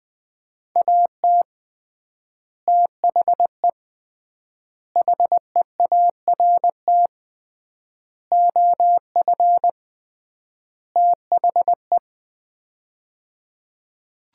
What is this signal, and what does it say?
Morse code, heard as a keyed tone, about 21 wpm, 700 Hz. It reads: AT THE HEART OF THE
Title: 3x Word-Spacing - Sets of 5 Words - 20wpm